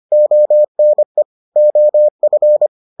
Morse code, heard as a keyed tone, about 25 wpm, 600 Hz. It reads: ONE OF